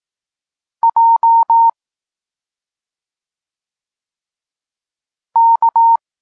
Morse code, heard as a keyed tone, about 18 wpm, 920 Hz. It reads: J K